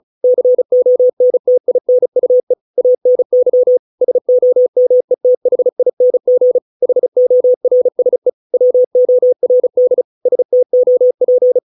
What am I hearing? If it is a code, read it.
Morse code, heard as a keyed tone, about 35 wpm, 500 Hz. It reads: CONTINUE ANY SOMETHING HORSE WORD STOP